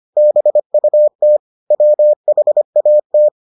Morse code, heard as a keyed tone, about 25 wpm, 600 Hz. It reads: BUT WHAT